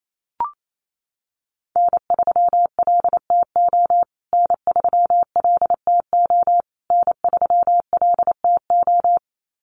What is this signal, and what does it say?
Morse code, heard as a keyed tone, about 28 wpm, 700 Hz. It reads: N3LTO N3LTO N3LTO
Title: USA Call Signs Repeated 3x 28wpm